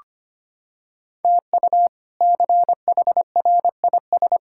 Morse code, heard as a keyed tone, about 25 wpm, 700 Hz. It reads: TU CHRIS
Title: K1USN SST QSO Practice - 25wpm